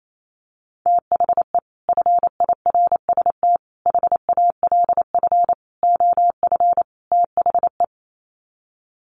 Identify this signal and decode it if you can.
Morse code, heard as a keyed tone, about 28 wpm, 700 Hz. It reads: THE FIRST HALF OF THE
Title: Sets of 5 Words 28wpm